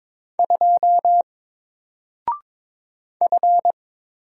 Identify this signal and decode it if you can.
Morse code, heard as a keyed tone, about 22 wpm, 700 Hz. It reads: 2 F
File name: Single-Letter-Number-22wpm